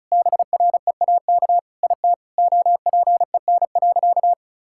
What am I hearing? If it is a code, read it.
Morse code, heard as a keyed tone, about 36 wpm, 700 Hz. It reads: BREAK IT OPEN.
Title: Sentences from Top 1000 Words - Review Part 6 - 35wpm